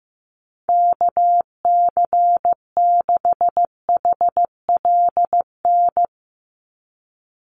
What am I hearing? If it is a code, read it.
Morse code, heard as a keyed tone, about 15 wpm, 700 Hz. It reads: KC6HLN
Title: USA Call Signs 15wpm